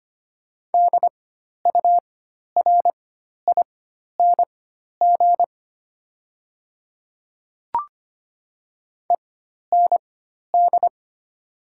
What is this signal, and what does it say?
Morse code, heard as a keyed tone, about 25 wpm, spaced as for 12 wpm, 700 Hz. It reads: DURING END
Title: 4x Character Spacing - Top 500 Words - 25wpm